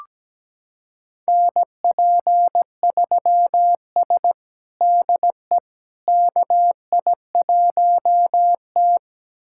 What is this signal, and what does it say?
Morse code, heard as a keyed tone, about 17 wpm, 700 Hz. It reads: NP3S DE KI1T
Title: USA Callsign DE Callsign - 4 Characters - 17wpm